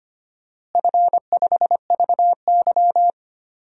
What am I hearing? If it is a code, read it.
Morse code, heard as a keyed tone, about 25 wpm, 700 Hz. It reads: F5VY
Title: DXCC-100 Call Signs - 4 Characters - Repeated 3x - 25wpm